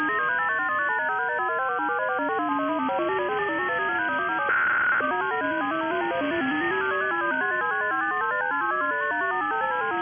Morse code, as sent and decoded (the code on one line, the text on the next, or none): none